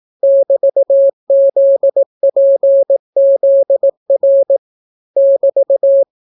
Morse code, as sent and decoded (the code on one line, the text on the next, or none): -...- --.. .--. --.. .-. / -...-
<BT>ZPZR <BT>